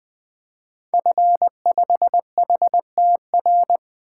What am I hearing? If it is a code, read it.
Morse code, heard as a keyed tone, about 20 wpm, 700 Hz. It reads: F5HTR
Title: 2023 SOTA Activator Call Signs - 20wpm